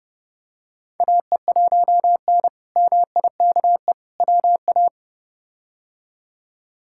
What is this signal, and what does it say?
Morse code, heard as a keyed tone, about 29 wpm, 700 Hz. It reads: AE1N MIKE WA
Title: K1USN SST QSO Practice - 30wpm